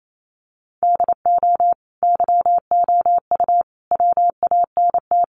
Morse code, heard as a keyed tone, about 28 wpm, 700 Hz. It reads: DO YOU WANT
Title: Sentences from Top 400 Words 28wpm